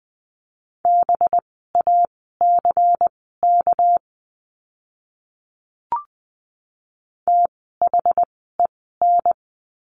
Morse code, heard as a keyed tone, about 20 wpm, spaced as for 15 wpm, 700 Hz. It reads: BACK THEN
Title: 230 Lesson 37f - Letter x - Words 20wpm